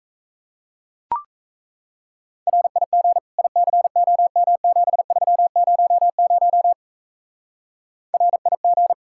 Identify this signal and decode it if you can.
Morse code, heard as a keyed tone, about 35 wpm, 700 Hz. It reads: RIG ICOM7300 RIG
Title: Speed-Racing Common QSO Phrases Encore - 28wpm